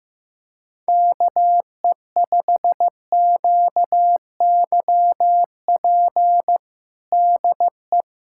Morse code, heard as a keyed tone, about 15 wpm, 700 Hz. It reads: KE5QYP DE
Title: USA Callsign DE Callsign - 6 Characters - 15wpm